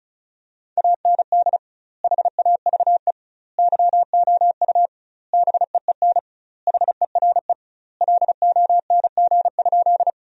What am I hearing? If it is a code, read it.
Morse code, heard as a keyed tone, about 34 wpm, 700 Hz. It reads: AND HAVE YOU BEEN HERE LONG?
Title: 2x Word-Spacing - Sentences from Top 200 Words - 35wpm